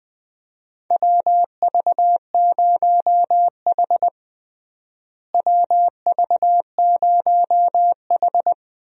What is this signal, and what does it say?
Morse code, heard as a keyed tone, about 20 wpm, 700 Hz. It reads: WV0H WV0H
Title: USA Call Signs - 4 Characters - Repeated 3x - 20wpm